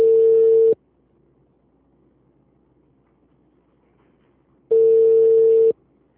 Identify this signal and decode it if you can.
Ringback tone (China, 450 Hz).
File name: out- -2006- - - .126